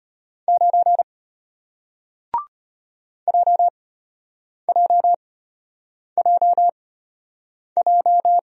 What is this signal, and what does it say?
Morse code, heard as a keyed tone, about 34 wpm, spaced as for 8 wpm, 700 Hz. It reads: 9 JJJJ